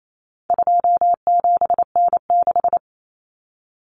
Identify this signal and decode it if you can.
Morse code, heard as a keyed tone, about 28 wpm, 700 Hz. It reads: 27N6
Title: Common Cut Numbers - 4-Digit 28wpm